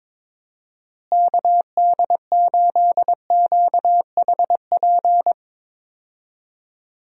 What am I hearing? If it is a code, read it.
Morse code, heard as a keyed tone, about 22 wpm, 700 Hz. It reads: KD8QHP
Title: USA Call Signs - 6 Characters - Repeated 2x - 22wpm